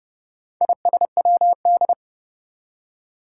Morse code, heard as a keyed tone, about 30 wpm, 700 Hz. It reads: ISWD
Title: Four Challenging Letter-Number Combinations 30wpm